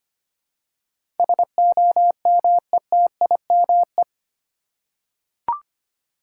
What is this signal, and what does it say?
Morse code, heard as a keyed tone, about 25 wpm, 700 Hz. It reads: SOMETIME
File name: 8 Letter Words - First 3 Letters Given - 25wpm